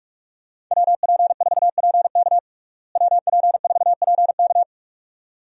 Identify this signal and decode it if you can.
Morse code, heard as a keyed tone, about 44 wpm, 700 Hz. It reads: WP4PK WP4PK
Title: USA Call Signs - 5 Characters - Repeated 2x - 45wpm